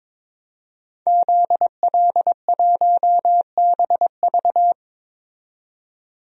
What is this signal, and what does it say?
Morse code, heard as a keyed tone, about 22 wpm, 700 Hz. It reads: ZL1BV
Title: DXCC-100 Call Signs - 5 Characters - 22wpm